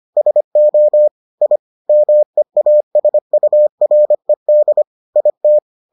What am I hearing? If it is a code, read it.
Morse code, heard as a keyed tone, about 25 wpm, 600 Hz. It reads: SO I MEASURED IT